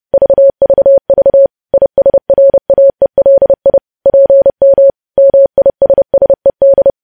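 Morse code, heard as a keyed tone, about 30 wpm, 550 Hz. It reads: VVV ISRAELI PM MISSED
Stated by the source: Morse code practice of news headlines sent at 30 wpm.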